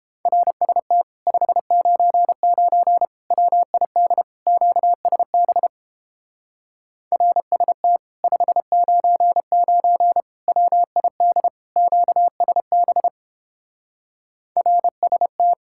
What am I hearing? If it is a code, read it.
Morse code, heard as a keyed tone, about 29 wpm, 700 Hz. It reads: RST 599 WID QSB RST 599 WID QSB RST